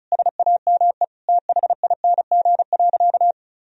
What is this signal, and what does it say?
Morse code, heard as a keyed tone, about 36 wpm, 700 Hz. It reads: SAME THING.
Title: Sentences from Top 1000 Words - Review Part 1 - 35wpm